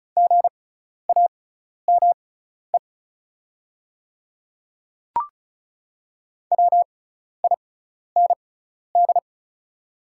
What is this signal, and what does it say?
Morse code, heard as a keyed tone, about 36 wpm, spaced as for 12 wpm, 700 Hz. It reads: GAME WIND
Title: 6x Character Spacing - Top 500 Words - 35wpm